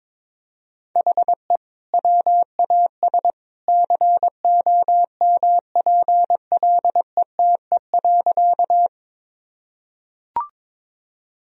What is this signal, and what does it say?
Morse code, heard as a keyed tone, about 22 wpm, 700 Hz. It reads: HE WAS COMPLETE.
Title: Sentences from Top 1000 Words - Review Part 2 - 22wpm